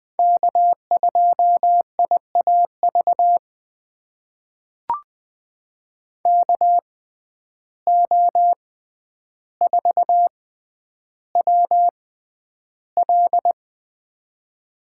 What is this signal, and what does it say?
Morse code, heard as a keyed tone, about 20 wpm, spaced as for 10 wpm, 700 Hz. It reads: K2IAV KO4WL
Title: USA Call Signs - 5 Characters - 6x Character Spacing - 20wpm